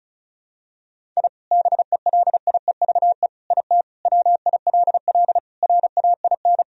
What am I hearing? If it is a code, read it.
Morse code, heard as a keyed tone, about 36 wpm, 700 Hz. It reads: I BELIEVE IT WILL RAIN
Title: Sentences from Top 1000 Words - Review Part 1 - 35wpm